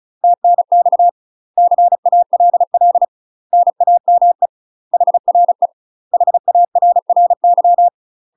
Morse code, heard as a keyed tone, about 36 wpm, 700 Hz. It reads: TNX CALL NAME HRE HARRY
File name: UK MASTER QSOs TRAINING / 35WPM / UK MASTER QSOs 0017